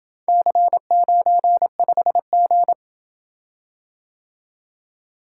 Morse code, heard as a keyed tone, about 27 wpm, 700 Hz. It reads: C95G